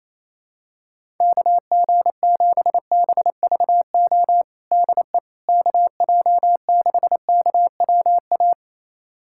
Morse code, heard as a keyed tone, about 28 wpm, 700 Hz. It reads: KG7BVO DE KJ6KWA
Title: USA Callsign DE Callsign - 6 Characters - 28wpm